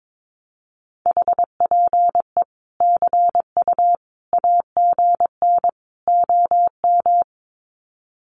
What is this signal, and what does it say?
Morse code, heard as a keyed tone, about 22 wpm, 700 Hz. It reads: HPE CU AGN OM
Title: Common QSO Phrases 22wpm